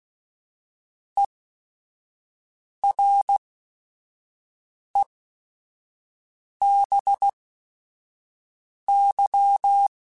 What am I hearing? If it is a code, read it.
Morse code, heard as a keyed tone, about 16 wpm, spaced as for 5 wpm, 800 Hz. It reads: EREBY
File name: Technical Subjects / Technical Subjects-mp3s / XC @5